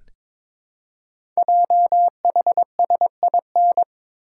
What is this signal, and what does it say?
Morse code, heard as a keyed tone, about 22 wpm, 700 Hz. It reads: JHSIN
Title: Code Groups - 5 Letters - 22wpm